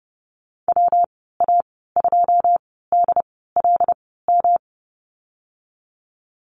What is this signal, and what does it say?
Morse code, heard as a keyed tone, about 30 wpm, spaced as for 17 wpm, 700 Hz. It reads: WA2DLM
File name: Lesson 19g - Letter w - Call Signs 30wpm